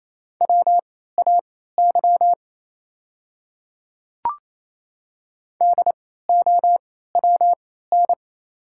Morse code, heard as a keyed tone, about 27 wpm, spaced as for 16 wpm, 690 Hz. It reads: WAY DOWN
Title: 3x Character Spacing - Top 100 Words - 28wpm